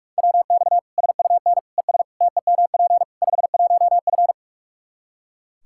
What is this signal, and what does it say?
Morse code, heard as a keyed tone, about 41 wpm, 700 Hz. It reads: WX SUN ES TEMP 51F
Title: Practice QSO - Exchange 2 - Info - 45wpm